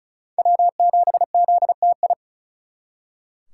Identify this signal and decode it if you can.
Morse code, heard as a keyed tone, about 48 wpm, spaced as for 42 wpm, 700 Hz. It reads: W7ZTI
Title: Speed Racing - USA 1x3 Call Signs 35wpm